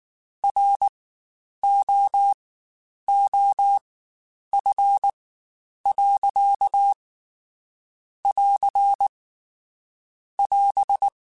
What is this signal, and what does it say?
Morse code, heard as a keyed tone, about 19 wpm, spaced as for 8 wpm, 800 Hz. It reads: ROOF.<AR><AS>
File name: Spaced Characters / Spaced Characters-mp3s / Weather 4 24 sp @18